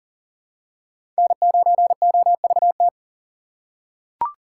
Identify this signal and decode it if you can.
Morse code, heard as a keyed tone, about 40 wpm, 700 Hz. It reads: N9OVT
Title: USA Call Signs - 5 Characters - 3x Character Spacing - 40wpm